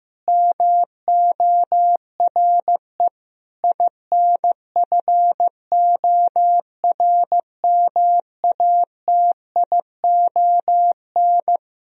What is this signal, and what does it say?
Morse code, heard as a keyed tone, about 15 wpm, 700 Hz. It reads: MORE INFORMATION
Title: Sentences from Top 1000 Words - Review Part 3 - 15wpm